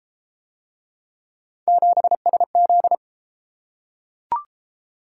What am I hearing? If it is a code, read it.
Morse code, heard as a keyed tone, about 31 wpm, 700 Hz. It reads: 7SZ